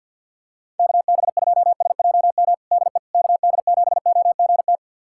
Morse code, heard as a keyed tone, about 47 wpm, 700 Hz. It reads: KB2SJM DE KD7OGT
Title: USA Callsign DE Callsign - 6 Characters - 50wpm